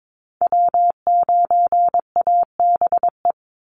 Morse code, heard as a keyed tone, about 22 wpm, 700 Hz. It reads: W9ABE
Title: USA Call Signs 22wpm